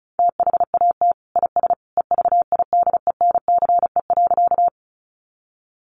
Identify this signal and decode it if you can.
Morse code, heard as a keyed tone, about 34 wpm, 700 Hz. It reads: THAT IS EVIDENCE.
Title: Sentences from Top 1300 Words 35wpm